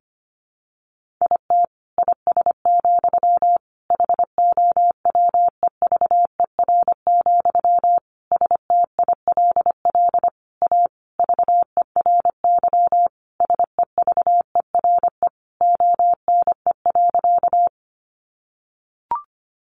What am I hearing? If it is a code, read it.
Morse code, heard as a keyed tone, about 25 wpm, 700 Hz. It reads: IT IS, HOWEVER, STILL A VERY SEVERE ONE.
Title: Sentences from Top 1400 Words 25wpm